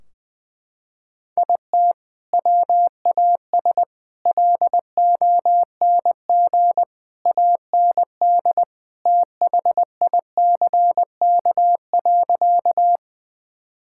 Morse code, heard as a keyed tone, about 20 wpm, 700 Hz. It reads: IT WAS LONG AND THICK.
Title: Sentences from Top 1600 Words 20wpm